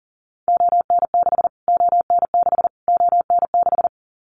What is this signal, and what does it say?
Morse code, heard as a keyed tone, about 40 wpm, 700 Hz. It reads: ON6 ON6 ON6
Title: Callsign Prefixes for Top 25 DXCC - Repeated 3x - 40wpm